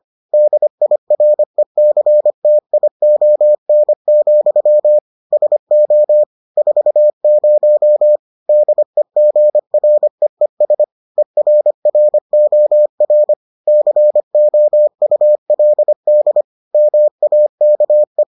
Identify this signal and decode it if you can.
Morse code, heard as a keyed tone, about 25 wpm, 600 Hz. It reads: DIRECTION, SO 40 DEGREES ERROR COULD MAKE